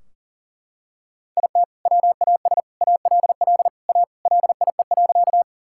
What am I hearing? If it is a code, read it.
Morse code, heard as a keyed tone, about 40 wpm, 700 Hz. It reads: IT WAS ALL A LIE.
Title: Sentences from Top 1600 Words 40wpm